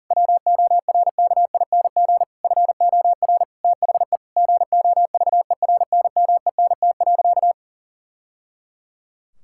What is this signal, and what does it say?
Morse code, heard as a keyed tone, about 40 wpm, 700 Hz. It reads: WORKING FOR THE GOVERNMENT.
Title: Sentences from Top 1000 Words - Review Part 6 - 40wpm